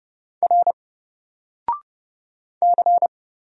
Morse code, heard as a keyed tone, about 30 wpm, 700 Hz. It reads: R C
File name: Single-Letter-Number-30wpm